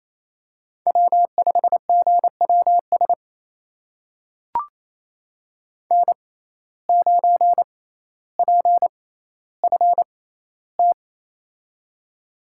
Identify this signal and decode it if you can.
Morse code, heard as a keyed tone, about 28 wpm, spaced as for 14 wpm, 700 Hz. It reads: W5GWS N9PFT
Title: USA Call Signs - 5 Characters - 6x Character Spacing - 28wpm